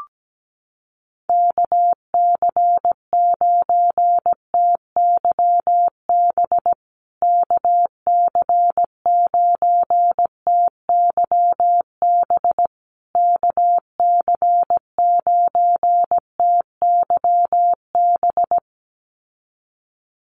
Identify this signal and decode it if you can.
Morse code, heard as a keyed tone, about 17 wpm, 700 Hz. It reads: KC9TYB KC9TYB KC9TYB
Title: USA Call Signs Repeated 3x 17wpm